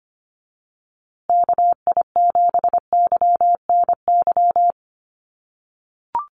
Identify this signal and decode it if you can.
Morse code, heard as a keyed tone, about 25 wpm, 700 Hz. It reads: KI7YNY
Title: USA Call Signs Repeated 3x 25wpm